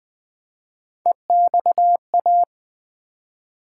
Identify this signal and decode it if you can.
Morse code, heard as a keyed tone, about 20 wpm, 700 Hz. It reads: EXA
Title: Build A Word - 1 Letter At A Time - 5-Letter Words - 20wpm